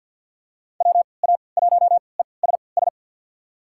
Morse code, heard as a keyed tone, about 47 wpm, spaced as for 28 wpm, 700 Hz. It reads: WA1ESS
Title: USA Call Signs - 6 Characters - 3x Character Spacing - 50wpm